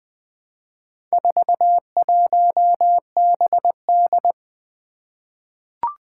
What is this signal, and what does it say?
Morse code, heard as a keyed tone, about 20 wpm, 700 Hz. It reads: 41BD
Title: Four Challenging Letter-Number Combinations 20wpm